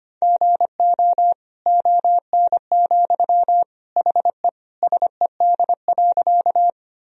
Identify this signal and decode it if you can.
Morse code, heard as a keyed tone, about 25 wpm, 700 Hz. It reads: GO ON, HE SED.
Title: Sentences from Top 100 Words and Abbreviations 25wpm